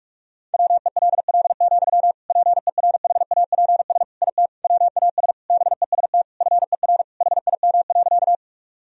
Morse code, heard as a keyed tone, about 47 wpm, 700 Hz. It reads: WELL, PERHAPS IT WAS BEST FER HIM.
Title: Sentences from Top 500 Words and Abbreviations 45wpm